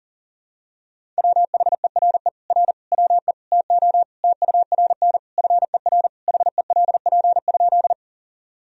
Morse code, heard as a keyed tone, about 40 wpm, 700 Hz. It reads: WHERE R WE TO TURN FER HELP?
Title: Sentences from Top 200 Words and Abbreviations 40wpm